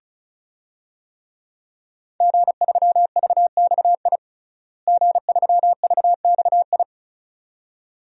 Morse code, heard as a keyed tone, about 36 wpm, 700 Hz. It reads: G3VXI G3VXI
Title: DXCC-100 Call Signs - 5 Characters - Repeated 2x - 35wpm